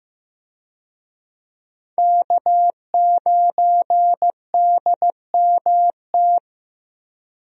Morse code, heard as a keyed tone, about 15 wpm, 700 Hz. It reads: K9DMT
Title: USA Call Signs - 5 Characters - 15wpm